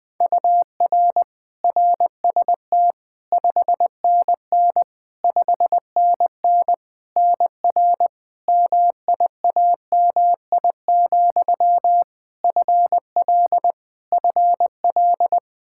Morse code, heard as a keyed tone, about 20 wpm, 700 Hz. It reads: UR RST 5NN 5NN NR MIAMI, FL FL
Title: K1USN SST QSO Practice 20wpm